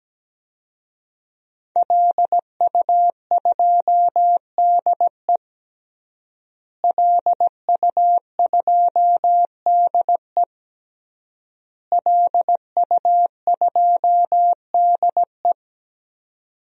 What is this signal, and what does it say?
Morse code, heard as a keyed tone, about 17 wpm, 700 Hz. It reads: LU2DE LU2DE LU2DE
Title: DXCC-100 Call Signs - 5 Characters - Repeated 3x - 17wpm